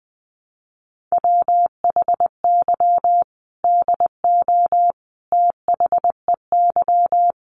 Morse code, heard as a keyed tone, about 20 wpm, 700 Hz. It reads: WHY DO THEY
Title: Sentences from Top 400 Words 20wpm